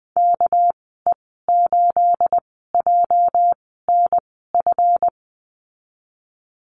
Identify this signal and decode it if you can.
Morse code, heard as a keyed tone, about 19 wpm, spaced as for 14 wpm, 700 Hz. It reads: KE8JNF
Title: 231 Lesson 37g - Letter x - Call Signs 20wpm